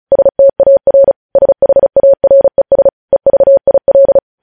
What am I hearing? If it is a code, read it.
Morse code, heard as a keyed tone, about 36 wpm, 550 Hz. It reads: STAR SHARES EVIL
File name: podcast / News headlines-1x-35x35